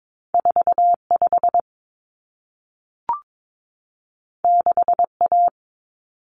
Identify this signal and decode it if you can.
Morse code, heard as a keyed tone, about 22 wpm, 700 Hz. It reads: 45 6A
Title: Common Cut Numbers - 2-Digit 22wpm